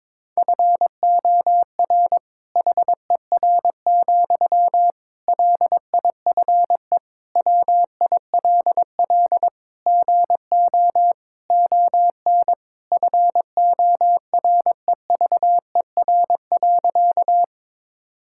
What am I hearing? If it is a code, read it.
Morse code, heard as a keyed tone, about 22 wpm, 700 Hz. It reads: FOR HER, LIFE WILL GO ON FOREVER.
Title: Sentences from Top 1600 Words 22wpm